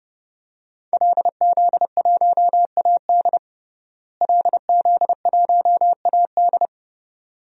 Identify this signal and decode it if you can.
Morse code, heard as a keyed tone, about 32 wpm, 700 Hz. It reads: LZ1AD LZ1AD
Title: DXCC-100 Call Signs - 5 Characters - Repeated 2x - 30wpm